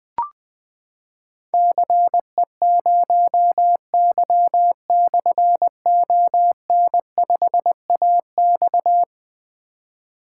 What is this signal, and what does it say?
Morse code, heard as a keyed tone, about 20 wpm, 700 Hz. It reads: CE0Y/ON5AX
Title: Difficult Call Signs 20wpm